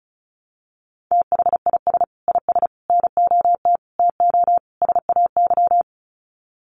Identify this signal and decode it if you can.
Morse code, heard as a keyed tone, about 34 wpm, 700 Hz. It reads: THIS IS NOT TO SAY
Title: Sets of 5 Words Encore 35wpm